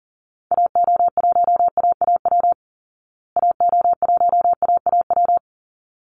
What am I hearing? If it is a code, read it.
Morse code, heard as a keyed tone, about 40 wpm, 700 Hz. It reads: AO1AAW AO1AAW
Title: Call Signs Repeated 3x - 40wpm